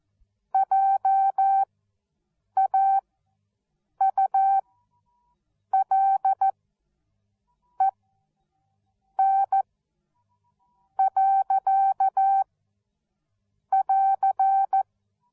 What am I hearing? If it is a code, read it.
Morse code, heard as a keyed tone, about 14 wpm, spaced as for 6 wpm, 780 Hz. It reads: JAULEN.<AR>
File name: Onlinelehrgang / morsen / DJ4UF Morsen13